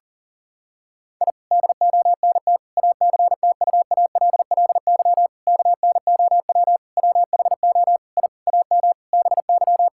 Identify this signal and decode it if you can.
Morse code, heard as a keyed tone, about 40 wpm, 700 Hz. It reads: I DONT ACTUALLY KNOW WHO I AM BY
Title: Sentences from Top 1000 Words - Review Part 5 - 40wpm